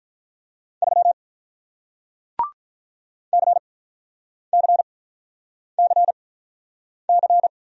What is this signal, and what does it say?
Morse code, heard as a keyed tone, about 47 wpm, spaced as for 9 wpm, 700 Hz. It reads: 3 CCCC